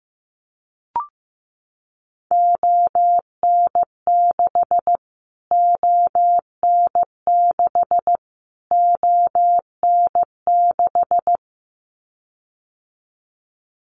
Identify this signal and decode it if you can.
Morse code, heard as a keyed tone, about 15 wpm, 700 Hz. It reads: ON6 ON6 ON6
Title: Callsign Prefixes for Top 25 DXCC - Repeated 3x - 15wpm